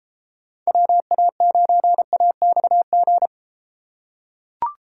Morse code, heard as a keyed tone, about 30 wpm, 700 Hz. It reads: WA9AXG